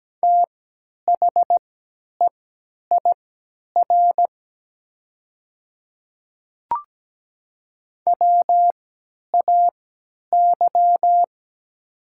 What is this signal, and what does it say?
Morse code, heard as a keyed tone, about 17 wpm, spaced as for 10 wpm, 690 Hz. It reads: THEIR WAY
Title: 3x Character Spacing - Top 100 Words - 17wpm